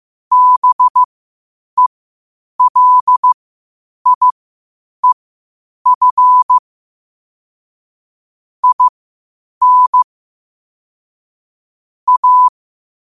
Morse code, heard as a keyed tone, about 15 wpm, spaced as for 8 wpm, 1000 Hz. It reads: BELIEF IN A
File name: qotdfiles / 8wpm